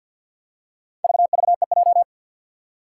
Morse code, heard as a keyed tone, about 50 wpm, 700 Hz. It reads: VVEJ
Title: Four Challenging Letter-Number Combinations 50wpm